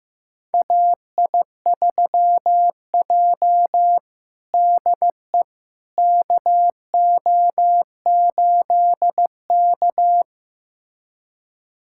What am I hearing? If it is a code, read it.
Morse code, heard as a keyed tone, about 15 wpm, 700 Hz. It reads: AI3J DE KO8K
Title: USA Callsign DE Callsign - 4 Characters - 15wpm